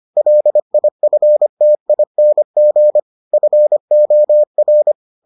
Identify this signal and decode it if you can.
Morse code, heard as a keyed tone, about 25 wpm, 600 Hz. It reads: LIFTING FOR